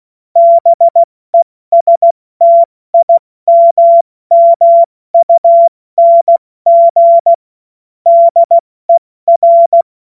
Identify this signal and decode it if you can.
Morse code, heard as a keyed tone, about 16 wpm, 680 Hz. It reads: BESTIMMUNG DER
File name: Morsekurs / Witze / Witz 16. Dezember 2025 70 F80